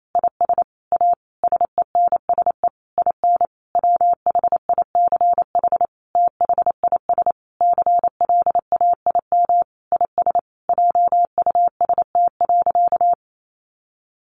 Morse code, heard as a keyed tone, about 28 wpm, 700 Hz. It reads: IS A SENSE IN WHICH THIS CLAIM IS JUST.
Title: Sentences from Top 1500 Words 28wpm